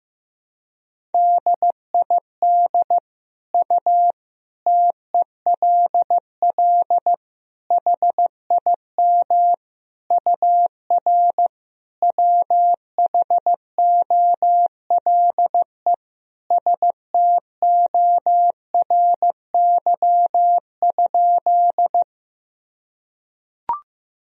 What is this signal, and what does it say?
Morse code, heard as a keyed tone, about 15 wpm, 700 Hz. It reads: DID U TELL HIM UR WHOLE STORY?
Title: Sentences from Top 400 Words and Abbreviations 15wpm